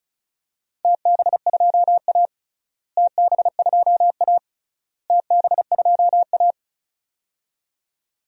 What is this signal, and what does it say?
Morse code, heard as a keyed tone, about 35 wpm, 700 Hz. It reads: TB2A TB2A TB2A
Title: DXCC-100 Call Signs - 4 Characters - Repeated 3x - 35wpm